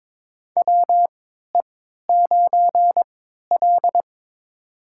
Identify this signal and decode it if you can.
Morse code, heard as a keyed tone, about 22 wpm, spaced as for 13 wpm, 700 Hz. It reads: WE9L
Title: USA Call Signs - 3x Character Spacing - 22wpm